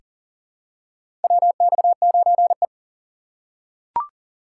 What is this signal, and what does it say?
Morse code, heard as a keyed tone, about 40 wpm, 700 Hz. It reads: WX9E
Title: USA Call Signs - 4 Characters - 3x Character Spacing - 40wpm